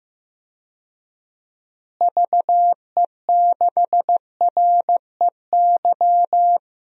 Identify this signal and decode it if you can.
Morse code, heard as a keyed tone, about 15 wpm, 700 Hz. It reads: VE6REY
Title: DXCC-100 Call Signs - 6 Characters - Repeated 3x - 15wpm